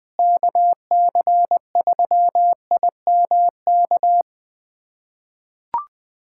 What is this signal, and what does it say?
Morse code, heard as a keyed tone, about 20 wpm, 700 Hz. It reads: KC3IMK
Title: USA Callsign DE Callsign - 6 Characters - 20wpm